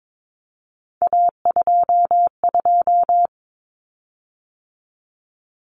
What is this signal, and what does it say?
Morse code, heard as a keyed tone, about 22 wpm, 700 Hz. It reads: A22
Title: Common Cut Numbers - 3-Digit 22wpm